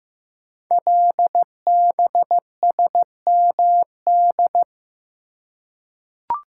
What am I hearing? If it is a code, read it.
Morse code, heard as a keyed tone, about 15 wpm, 700 Hz. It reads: LBSMD
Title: Code Groups - 5 Letters - 15wpm